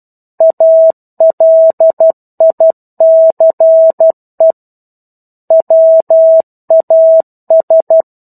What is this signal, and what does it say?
Morse code, heard as a keyed tone, about 12 wpm, 650 Hz. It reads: ALICE WAS
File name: 01-Chapter-01 12 WPM